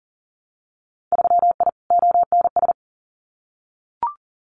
Morse code, heard as a keyed tone, about 40 wpm, 700 Hz. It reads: 3I ONS
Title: Winter Field Day Exchanges Encore 40wpm